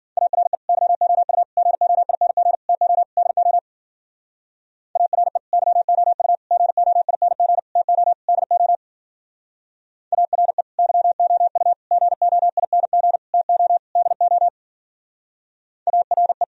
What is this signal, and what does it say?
Morse code, heard as a keyed tone, about 51 wpm, 700 Hz. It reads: ARE YOU GOING TO DO ARE YOU GOING TO DO ARE YOU GOING TO DO ARE